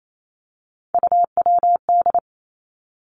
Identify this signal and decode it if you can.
Morse code, heard as a keyed tone, about 28 wpm, 700 Hz. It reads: UWD
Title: Three Challenging Letter-Number Combinations 28wpm